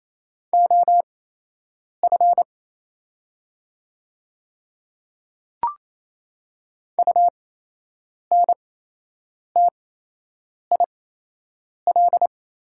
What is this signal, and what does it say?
Morse code, heard as a keyed tone, about 28 wpm, spaced as for 8 wpm, 700 Hz. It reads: OF UNTIL
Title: 8x Character Spacing - Top 500 Words - 28wpm